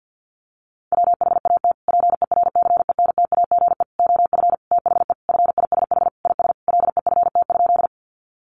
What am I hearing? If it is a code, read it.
Morse code, heard as a keyed tone, about 50 wpm, 700 Hz. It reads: WHAT PERCENTAGE OF THE FISH IS LEFT?
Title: Sentences from Top 1300 Words 50wpm